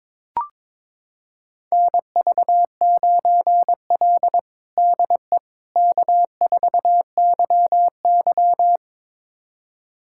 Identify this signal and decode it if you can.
Morse code, heard as a keyed tone, about 22 wpm, 700 Hz. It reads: NV9L DE K4YY
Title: USA Callsign DE Callsign - 4 Characters - 22wpm